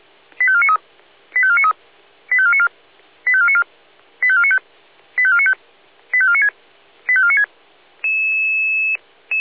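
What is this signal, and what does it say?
Five-tone selective call (ZVEI), heard as quick runs of five tones, tones of about 70 ms. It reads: 85493 85493 85494 85494 85495 85495 85496 85496
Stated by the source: ZVEI five-tone selective call received on an SDR.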